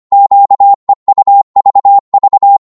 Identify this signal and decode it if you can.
Morse code, heard as a keyed tone, about 24 wpm, 800 Hz. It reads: QEUVV
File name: Morse / ltr-25wpm